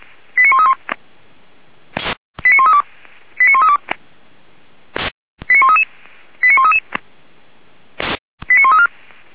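Five-tone selective call (ZVEI), heard as quick runs of five tones, tones of about 70 ms. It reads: 89131 89132 89132 89133 89133 89134
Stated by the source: ZVEI five-tone call received on SDR.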